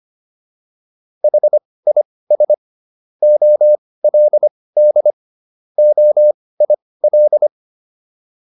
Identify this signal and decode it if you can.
Morse code, heard as a keyed tone, about 25 wpm, spaced as for 18 wpm, 600 Hz. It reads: HIS OLD OIL